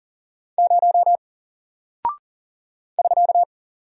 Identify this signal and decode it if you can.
Morse code, heard as a keyed tone, about 39 wpm, 700 Hz. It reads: 0 <SK>
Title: Single Character (Letter:Number:Punctuation:Prosign) Alt 40wpm